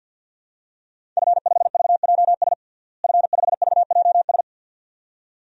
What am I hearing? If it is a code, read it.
Morse code, heard as a keyed tone, about 50 wpm, 700 Hz. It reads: F5VJS F5VJS
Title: DXCC-100 Call Signs - 5 Characters - Repeated 2x - 50wpm